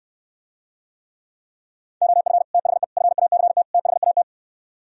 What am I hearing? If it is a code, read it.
Morse code, heard as a keyed tone, about 67 wpm, 700 Hz. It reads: OF THE FACT THAT